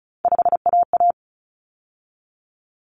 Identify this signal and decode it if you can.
Morse code, heard as a keyed tone, about 34 wpm, 700 Hz. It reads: 5AA
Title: Common Cut Numbers - 3-Digit 35wpm